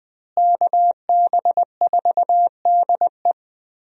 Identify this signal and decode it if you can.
Morse code, heard as a keyed tone, about 21 wpm, 700 Hz. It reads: KB4DE
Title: USA Call Signs - 5 Characters - Repeated 3x - 20wpm